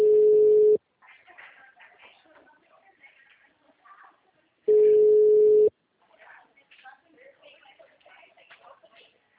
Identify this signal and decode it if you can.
Ringback tone (Europe, 425 Hz), with speech.
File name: out- -2016- - - .40